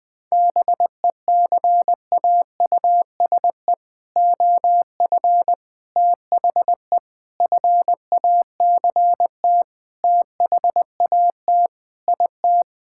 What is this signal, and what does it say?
Morse code, heard as a keyed tone, about 20 wpm, 700 Hz. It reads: BECAUSE OF THE FACT THAT IT
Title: Sentences with 4-Word Phrases - 20wpm